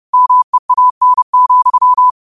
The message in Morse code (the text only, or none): MEAN,